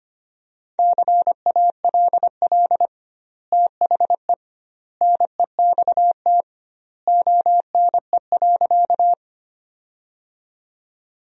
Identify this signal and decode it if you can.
Morse code, heard as a keyed tone, about 25 wpm, 700 Hz. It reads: CALL THE NEXT ONE.
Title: 2x Word-Spacing - Sentences from Top 300 Words - 25wpm